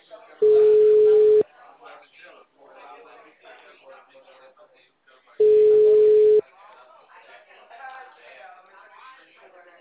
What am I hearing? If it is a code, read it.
Ringback tone (Europe, 425 Hz), with speech.